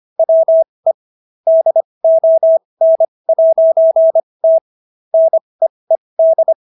Morse code, heard as a keyed tone, about 25 wpm, spaced as for 20 wpm, 650 Hz. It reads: WE DON'T NEED